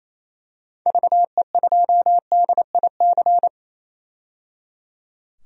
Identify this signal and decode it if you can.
Morse code, heard as a keyed tone, about 27 wpm, 700 Hz. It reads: VE2DIC
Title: DXCC-100 Call Signs - 6 Characters - 28wpm